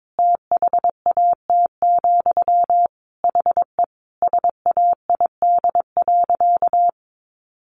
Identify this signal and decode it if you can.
Morse code, heard as a keyed tone, about 22 wpm, 700 Hz. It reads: THAT, HE SAID.
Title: Sentences from Top 1000 Words 22wpm